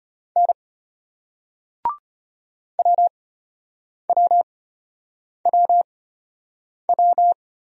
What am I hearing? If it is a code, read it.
Morse code, heard as a keyed tone, about 36 wpm, spaced as for 8 wpm, 700 Hz. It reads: N WWWW